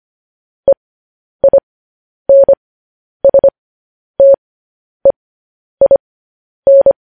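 Morse code, heard as a keyed tone, about 25 wpm, spaced as for 10 wpm, 550 Hz. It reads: EINSTEIN
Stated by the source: Morse code news headlines at 10.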